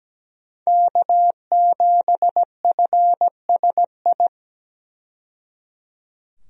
Morse code, heard as a keyed tone, about 17 wpm, 700 Hz. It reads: K7FSI
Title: USA Call Signs - 5 Characters - 17wpm